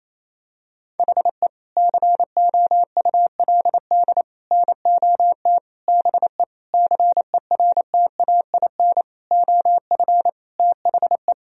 Morse code, heard as a keyed tone, about 28 wpm, 700 Hz. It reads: HE COULD NOT BE CERTAIN OF THE
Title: Sentences from Top 1000 Words - Review Part 2 - 28wpm